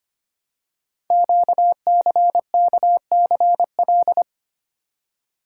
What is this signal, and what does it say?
Morse code, heard as a keyed tone, about 26 wpm, 700 Hz. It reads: QCKCL
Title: Code Groups - 5 Letters - 25wpm